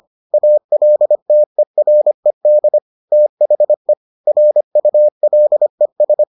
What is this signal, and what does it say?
Morse code, heard as a keyed tone, about 25 wpm, 600 Hz. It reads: ALTERED THE RULES